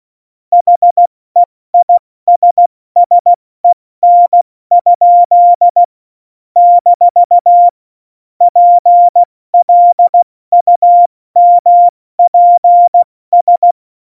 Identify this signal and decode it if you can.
Morse code, heard as a keyed tone, about 16 wpm, 700 Hz. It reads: HEISSEN? - PLUMPS
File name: Morsekurs / Witze / Witz 05. September 17, Tempo 70